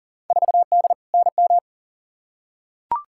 Morse code, heard as a keyed tone, about 40 wpm, 700 Hz. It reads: 4D NM
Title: Field Day Exchanges Encore - 40wpm